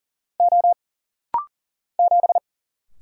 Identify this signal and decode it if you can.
Morse code, heard as a keyed tone, about 41 wpm, 700 Hz. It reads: O 7
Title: Single Letter-Number - Rapid-Fire 40wpm